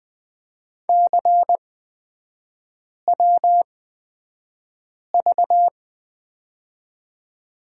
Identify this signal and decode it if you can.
Morse code, heard as a keyed tone, about 20 wpm, spaced as for 5 wpm, 690 Hz. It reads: CWV
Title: 229 Lesson 37e - Letter x - Three Characters 20wpm